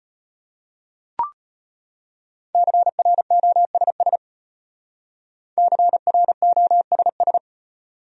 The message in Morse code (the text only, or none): CROSS CROSS